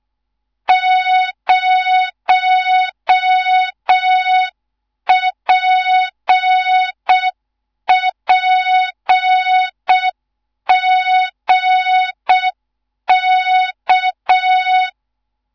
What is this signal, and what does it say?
Morse code, heard as a keyed tone, about 6 wpm, 3700 Hz. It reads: 0PPGK
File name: Morze-HA2MN / hang / 14 280 HA2MN